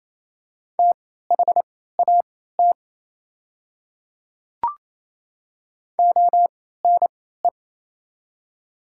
Morse code, heard as a keyed tone, about 28 wpm, spaced as for 16 wpm, 700 Hz. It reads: THAT ONE
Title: 3x Character Spacing - Top 200 Words - 28wpm